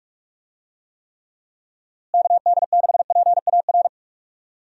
Morse code, heard as a keyed tone, about 46 wpm, 700 Hz. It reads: KD6PAR
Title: USA Call Signs - 6 Characters - Repeated 3x - 45wpm